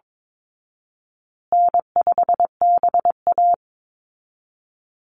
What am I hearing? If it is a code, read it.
Morse code, heard as a keyed tone, about 22 wpm, 700 Hz. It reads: N5BA
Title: Easy Call Signs Repeated 3x - 22wpm